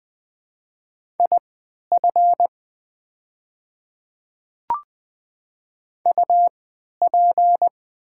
Morse code, heard as a keyed tone, about 20 wpm, spaced as for 11 wpm, 700 Hz. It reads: IF UP
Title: 3x Character Spacing - Top 300 Words - 20wpm